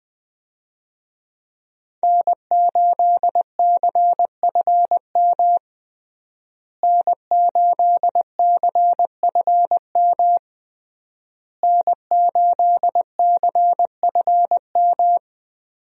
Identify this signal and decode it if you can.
Morse code, heard as a keyed tone, about 20 wpm, 700 Hz. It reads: N8CFM N8CFM N8CFM
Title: USA Call Signs - 5 Characters - Repeated 3x - 20wpm